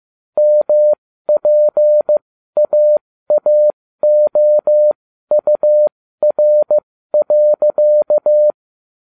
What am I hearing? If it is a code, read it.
Morse code, heard as a keyed tone, about 15 wpm, spaced as for 13 wpm, 600 Hz. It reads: MPAAOUR.